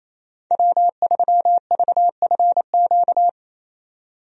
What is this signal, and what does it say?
Morse code, heard as a keyed tone, about 27 wpm, 700 Hz. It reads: W3VFQ
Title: USA Call Signs - 5 Characters - 28wpm